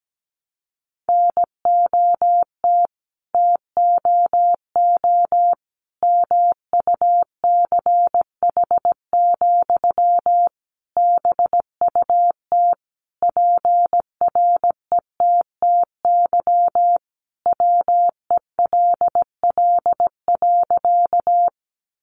Morse code, heard as a keyed tone, about 17 wpm, 700 Hz. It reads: NOT TOO MUCH, BUT PRETTY WELL.
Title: Sentences from Top 1000 Words 17wpm